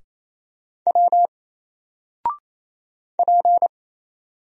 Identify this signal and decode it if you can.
Morse code, heard as a keyed tone, about 27 wpm, 700 Hz. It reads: W P
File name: Single-Letter-Number-28wpm